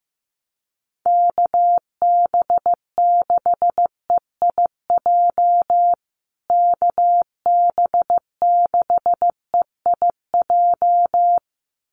Morse code, heard as a keyed tone, about 15 wpm, 700 Hz. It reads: KB6EIJ KB6EIJ
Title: USA Call Signs Repeated 2x 15wpm